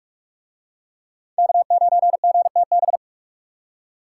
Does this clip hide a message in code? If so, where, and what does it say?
Yes: Morse code from 1.4 to 3.0 s, heard as a keyed tone, about 44 wpm, 700 Hz. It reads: K9GTB